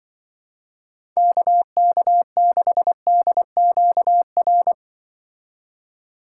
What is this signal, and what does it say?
Morse code, heard as a keyed tone, about 25 wpm, 700 Hz. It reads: KK6DQR
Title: Speed Racing - USA 2x3 Call Signs 20wpm